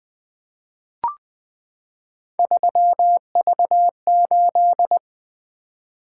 Morse code, heard as a keyed tone, about 20 wpm, 700 Hz. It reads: 3V8
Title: Three Letter-Number Combinations - 20wpm